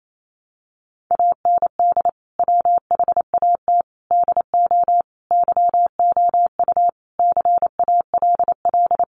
Morse code, heard as a keyed tone, about 28 wpm, 700 Hz. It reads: AND WHAT DO YOU CALL
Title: Sentences from Top 100 Words 28wpm